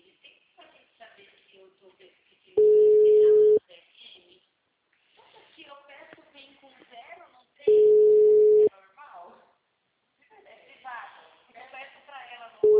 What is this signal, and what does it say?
Ringback tone (Europe, 425 Hz), with speech.